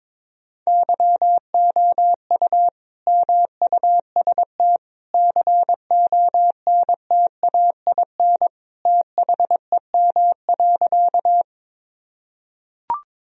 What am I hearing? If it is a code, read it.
Morse code, heard as a keyed tone, about 21 wpm, 700 Hz. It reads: YOU MUST CONTAIN THEM.
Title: Sentences from Top 1000 Words - Review Part 2 - 22wpm